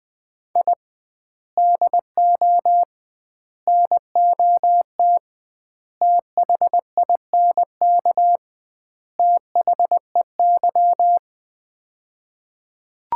Morse code, heard as a keyed tone, about 21 wpm, 700 Hz. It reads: I DO NOT THINK THEY
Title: 2x Word-Spacing - Sets of 5 Words - 20wpm